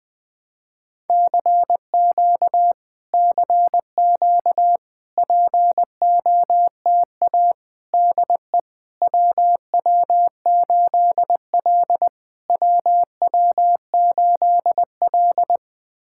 Morse code, heard as a keyed tone, about 21 wpm, 700 Hz. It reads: CQ CQ POTA DE WW8L WW8L
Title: POTA QSO Practice - 20wpm